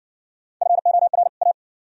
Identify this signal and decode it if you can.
Morse code, heard as a keyed tone, about 54 wpm, 700 Hz. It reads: FOR A